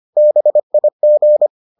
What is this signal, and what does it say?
Morse code, heard as a keyed tone, about 25 wpm, 600 Hz. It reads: BIG